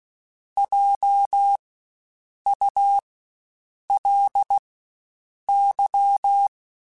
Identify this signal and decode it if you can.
Morse code, heard as a keyed tone, about 16 wpm, spaced as for 7 wpm, 800 Hz. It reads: JULY
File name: Spaced Alphas / Spaced Alphas mp3s / Js sp @15